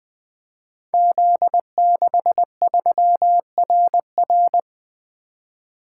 Morse code, heard as a keyed tone, about 20 wpm, 700 Hz. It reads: Z63RR
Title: Code Groups - 5 Letter:Numbers - 20wpm